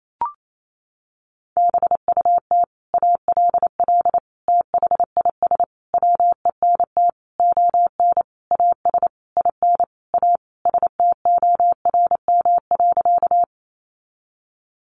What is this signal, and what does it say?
Morse code, heard as a keyed tone, about 28 wpm, 700 Hz. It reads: BUT ALL THIS WENT ON AS IN A STORM.
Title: Sentences from Top 1500 Words 28wpm